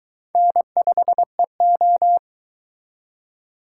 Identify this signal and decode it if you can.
Morse code, heard as a keyed tone, about 22 wpm, 700 Hz. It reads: N5EO